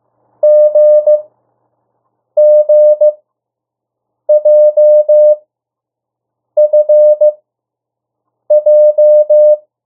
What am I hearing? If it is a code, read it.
Morse code, heard as a keyed tone, about 15 wpm, spaced as for 6 wpm, 600 Hz. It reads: GGJFJ
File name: morse mp3 / 02 urok 1